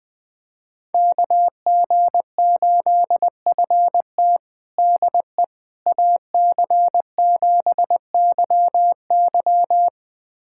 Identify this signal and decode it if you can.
Morse code, heard as a keyed tone, about 21 wpm, 700 Hz. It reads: KG8FT DE AC7YY
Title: USA Callsign DE Callsign - 5 Characters - 20wpm